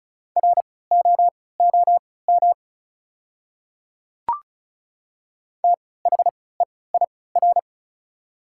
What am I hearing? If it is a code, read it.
Morse code, heard as a keyed tone, about 36 wpm, spaced as for 20 wpm, 700 Hz. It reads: ROOM THEIR
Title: 3x Character Spacing - Top 300 Words - 35wpm